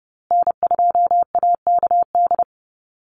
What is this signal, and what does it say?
Morse code, heard as a keyed tone, about 30 wpm, 700 Hz. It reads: N2AKD
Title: USA Call Signs Repeated 3x 30wpm